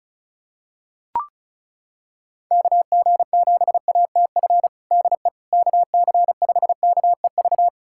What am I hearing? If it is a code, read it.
Morse code, heard as a keyed tone, about 34 wpm, 700 Hz. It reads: KG7ATF DE KC5KEV
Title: USA Callsign DE Callsign - 6 Characters - 35wpm